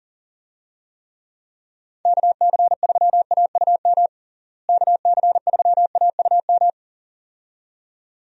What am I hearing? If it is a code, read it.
Morse code, heard as a keyed tone, about 37 wpm, 700 Hz. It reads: KC3AUM KC3AUM
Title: USA Call Signs - 6 Characters - Repeated 2x - 40wpm